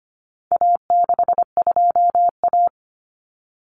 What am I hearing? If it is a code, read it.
Morse code, heard as a keyed tone, about 25 wpm, 700 Hz. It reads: A62A
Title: Common Cut Numbers - 4-Digit 25wpm